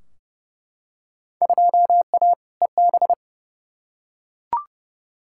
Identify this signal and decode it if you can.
Morse code, heard as a keyed tone, about 30 wpm, 700 Hz. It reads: 2A EB
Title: Field Day Exchanges - 30wpm